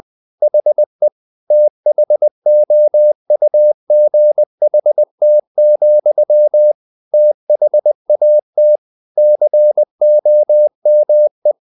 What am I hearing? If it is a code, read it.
Morse code, heard as a keyed tone, about 20 wpm, 600 Hz. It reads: HE THOUGHT, THAT COME
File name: the old man and the sea / the old man and the sea part 1 20wpm